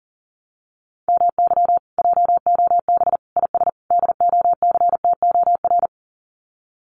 Morse code, heard as a keyed tone, about 40 wpm, 700 Hz. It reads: MY JOB IS DOCTOR
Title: Common QSO Phrases Encore 40wpm